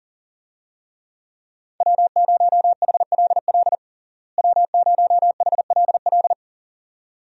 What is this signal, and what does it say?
Morse code, heard as a keyed tone, about 40 wpm, 700 Hz. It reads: W0HLL W0HLL
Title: USA Call Signs - 5 Characters - Repeated 2x - 40wpm